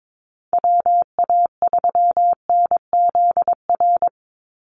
Morse code, heard as a keyed tone, about 22 wpm, 690 Hz. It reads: WA3NZR
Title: Call Signs Repeated 3x - 22wpm